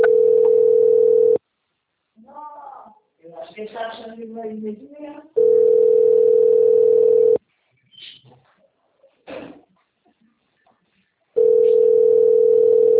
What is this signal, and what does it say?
Ringback tone (North America), 440 + 480 Hz, with speech.
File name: out- - - - - .20974